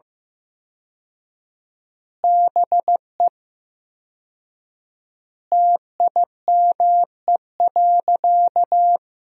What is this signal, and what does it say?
Morse code, heard as a keyed tone, about 15 wpm, 700 Hz. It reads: BE TIME.
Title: 4x Word-Spacing - Sentences from Top 200 Words - 15wpm